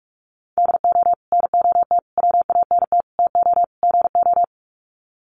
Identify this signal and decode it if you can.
Morse code, heard as a keyed tone, about 44 wpm, 700 Hz. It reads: DO NOT WANT TO GO
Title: Sets of 5 Words 45wpm